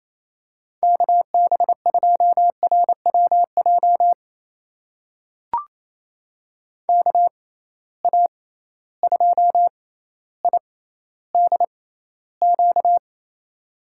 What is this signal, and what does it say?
Morse code, heard as a keyed tone, about 28 wpm, spaced as for 14 wpm, 700 Hz. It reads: KB2RWJ KA2IDQ
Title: USA Call Signs - 6 Characters - 6x Character Spacing - 28wpm